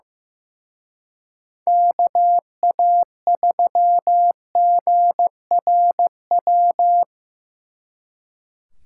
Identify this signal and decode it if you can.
Morse code, heard as a keyed tone, about 17 wpm, 700 Hz. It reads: KA3GRW
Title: Speed Racing - USA 2x3 Call Signs 15wpm